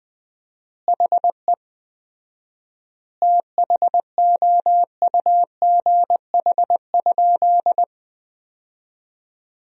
Morse code, heard as a keyed tone, about 20 wpm, 700 Hz. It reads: HE THOUGH?
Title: 4x Word-Spacing - Sentences from Top 400 Words - 20wpm